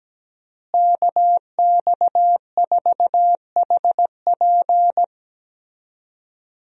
Morse code, heard as a keyed tone, about 17 wpm, 700 Hz. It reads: KX4HP
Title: USA Call Signs - 5 Characters - Repeated 3x - 17wpm